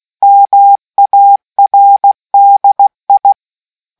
Morse code, heard as a keyed tone, about 16 wpm, 800 Hz. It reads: MARDI